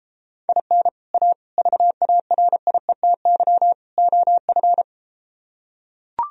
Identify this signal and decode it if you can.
Morse code, heard as a keyed tone, about 33 wpm, 700 Hz. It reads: IN A VARIETY OF